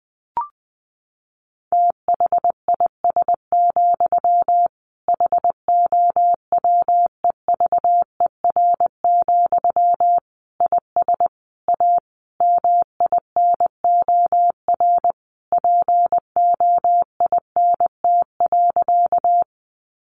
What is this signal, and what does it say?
Morse code, heard as a keyed tone, about 20 wpm, 700 Hz. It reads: THIS, HOWEVER, IS A MINOR POINT.
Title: Sentences from Top 1500 Words 20wpm